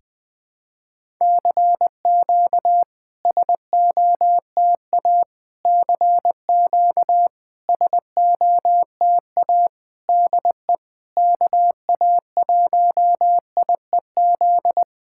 Morse code, heard as a keyed tone, about 20 wpm, 700 Hz. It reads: CQ SOTA CQ SOTA DE KA1IEZ
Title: SOTA QSO Practice 20wpm